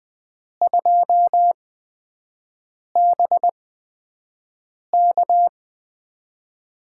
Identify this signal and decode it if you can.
Morse code, heard as a keyed tone, about 20 wpm, spaced as for 5 wpm, 700 Hz. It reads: 2BK
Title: USA Call Signs - 8x Character Spacing - 20wpm